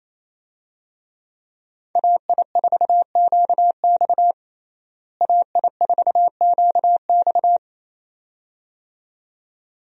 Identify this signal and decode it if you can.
Morse code, heard as a keyed tone, about 28 wpm, 700 Hz. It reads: AI4QX AI4QX
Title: USA Call Signs - 5 Characters - Repeated 2x - 28wpm